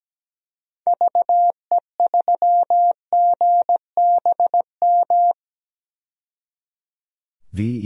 Morse code, heard as a keyed tone, about 17 wpm, 700 Hz. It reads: VE3GBM
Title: DXCC-100 Call Signs - 6 Characters - 17wpm